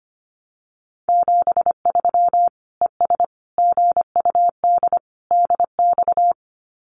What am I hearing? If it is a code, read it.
Morse code, heard as a keyed tone, about 25 wpm, 700 Hz. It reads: 73 ES GUD DX
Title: Common QSO Phrases 25wpm